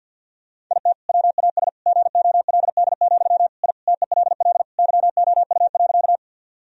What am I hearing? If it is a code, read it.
Morse code, heard as a keyed tone, about 50 wpm, 700 Hz. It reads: IT WAS GOLD, I TELL YOU.
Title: Sentences from Top 1000 Words - Review Part 3 - 50wpm